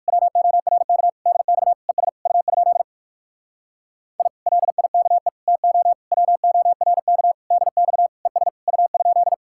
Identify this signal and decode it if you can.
Morse code, heard as a keyed tone, about 58 wpm, 700 Hz. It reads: WORK DX ES U? I LIKE TO WORK DX ES U?